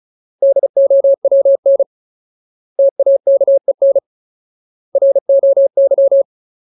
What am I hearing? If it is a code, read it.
Morse code, heard as a keyed tone, about 34 wpm, 550 Hz. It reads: DOWN TAKEN ROY
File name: Ultimate Word List / 35WPM Ultimate